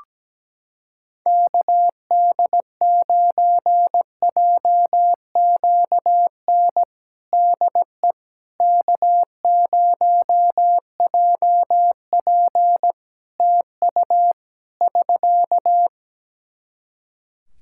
Morse code, heard as a keyed tone, about 17 wpm, 700 Hz. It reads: KD9JQN DE K0JP TU <SK>
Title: Practice QSO - Exchange 4 - Ending - 17wpm